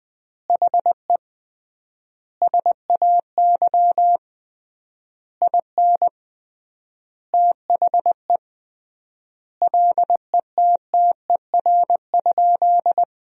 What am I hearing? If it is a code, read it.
Morse code, heard as a keyed tone, about 20 wpm, 700 Hz. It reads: HE SAY IN THE LETTER?
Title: 3x Word-Spacing - Sentences from Top 300 Words - 20wpm